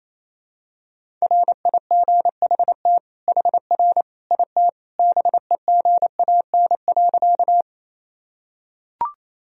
Morse code, heard as a keyed tone, about 28 wpm, 700 Hz. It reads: RIGHT HR IT BEGAN.
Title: Sentences from Top 300 Words and Abbreviations 28wpm